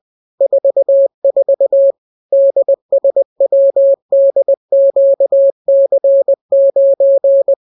Morse code, heard as a keyed tone, about 20 wpm, 550 Hz. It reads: 44 DSWDQC9